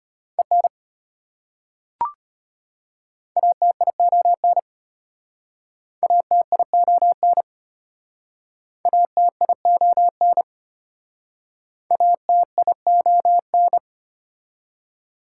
Morse code, heard as a keyed tone, about 33 wpm, 690 Hz. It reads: EN ATION ATION ATION ATION